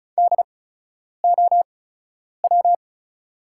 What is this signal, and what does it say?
Morse code, heard as a keyed tone, about 34 wpm, spaced as for 10 wpm, 700 Hz. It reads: DOW
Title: 8x Character Spacing - Top 300 Words - 35wpm